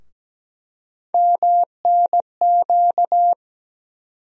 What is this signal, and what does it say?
Morse code, heard as a keyed tone, about 17 wpm, 700 Hz. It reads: MNQ
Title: USA Call Sign Postfixes - 3 Letters 17wpm